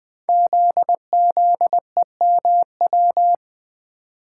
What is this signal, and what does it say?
Morse code, heard as a keyed tone, about 20 wpm, 700 Hz. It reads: ZZEMW
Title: Code Groups - 5 Letters - 20wpm